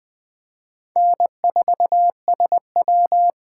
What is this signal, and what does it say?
Morse code, heard as a keyed tone, about 21 wpm, 700 Hz. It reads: N4SW
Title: Speed Racing - USA Call Signs 17wpm